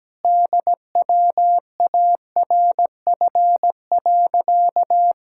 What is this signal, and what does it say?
Morse code, heard as a keyed tone, about 17 wpm, 700 Hz. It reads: DWARF.
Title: Pangram Challenge v2 - 17wpm